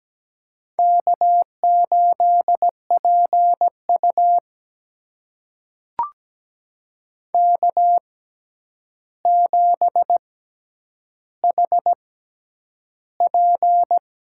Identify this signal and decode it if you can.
Morse code, heard as a keyed tone, about 17 wpm, spaced as for 9 wpm, 700 Hz. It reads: K8PU K7HP
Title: USA Call Signs - 4 Characters - 6x Character Spacing - 17wpm